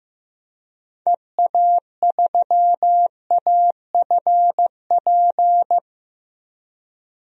Morse code, heard as a keyed tone, about 15 wpm, 700 Hz. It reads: EA3AFP
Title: DXCC-100 Call Signs - 6 Characters - Repeated 2x - 15wpm